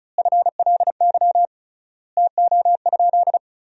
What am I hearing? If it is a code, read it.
Morse code, heard as a keyed tone, about 36 wpm, 700 Hz. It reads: FLY TO?
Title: 3x Word-Spacing - Sentences from Top 500 Words - 35wpm